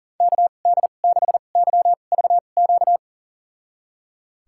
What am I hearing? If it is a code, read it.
Morse code, heard as a keyed tone, about 38 wpm, spaced as for 28 wpm, 700 Hz. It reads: KD6YVQ
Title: USA Call Signs - 6 Characters - 2x Character Spacing - 40wpm